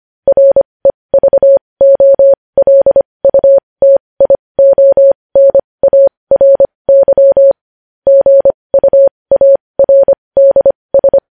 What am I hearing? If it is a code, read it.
Morse code, heard as a keyed tone, about 25 wpm, spaced as for 20 wpm, 550 Hz. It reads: REVOLUTIONARY GUARDS
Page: News headlines in Morse code at 20 wpm